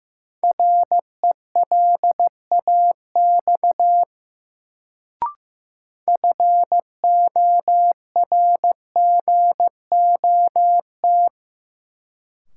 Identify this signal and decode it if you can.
Morse code, heard as a keyed tone, about 15 wpm, 700 Hz. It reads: RELAX FORGOT
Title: Sentences from Top 1400 Words - New Word Drill Alt - 15wpm